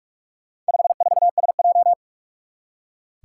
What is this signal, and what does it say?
Morse code, heard as a keyed tone, about 44 wpm, 700 Hz. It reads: 54SJ
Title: Four Challenging Letter-Number Combinations 45wpm